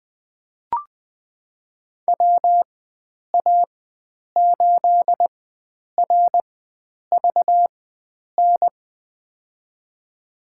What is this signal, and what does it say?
Morse code, heard as a keyed tone, about 20 wpm, spaced as for 9 wpm, 700 Hz. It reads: WA8RVN
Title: USA Call Signs - 6 Characters - 4x Character Spacing - 20wpm